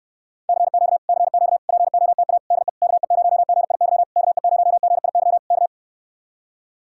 Morse code, heard as a keyed tone, about 69 wpm, 700 Hz. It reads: CQ CQ POTA DE KE0GIY KE0GIY K